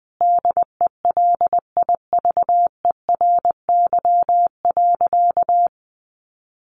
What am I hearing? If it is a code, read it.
Morse code, heard as a keyed tone, about 20 wpm, 700 Hz. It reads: DELIVERY.
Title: Sentences from Top 1500 Words 20wpm